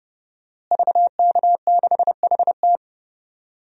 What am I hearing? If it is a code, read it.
Morse code, heard as a keyed tone, about 30 wpm, 700 Hz. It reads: VK6HT
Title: DXCC-100 Call Signs - 5 Characters - Repeated 3x - 30wpm